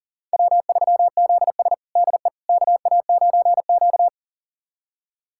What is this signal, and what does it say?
Morse code, heard as a keyed tone, about 40 wpm, 700 Hz. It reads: W3ZS DE KA9Q
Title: USA Callsign DE Callsign - 4 Characters - 40wpm